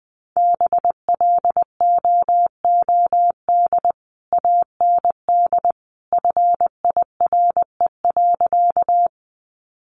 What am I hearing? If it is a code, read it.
Morse code, heard as a keyed tone, about 19 wpm, 700 Hz. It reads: BLOOD AND FIRE.
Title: Sentences from Top 1100 Words 20wpm